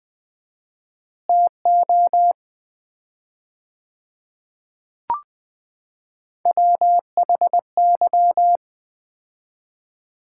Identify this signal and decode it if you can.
Morse code, heard as a keyed tone, about 20 wpm, 700 Hz. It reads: TO WHY
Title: 4x Word-Spacing - Sets of 5 Words - 20wpm